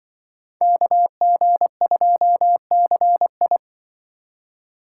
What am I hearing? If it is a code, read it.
Morse code, heard as a keyed tone, about 25 wpm, 700 Hz. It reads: KG2CI
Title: Speed Racing - USA Call Signs 20wpm